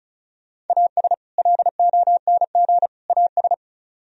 Morse code, heard as a keyed tone, about 44 wpm, 700 Hz. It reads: AS LONG AS